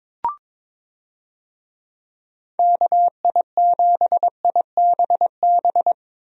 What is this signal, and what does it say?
Morse code, heard as a keyed tone, about 22 wpm, 700 Hz. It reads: KI7IBB
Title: USA Call Signs - 6 Characters - Repeated 2x - 22wpm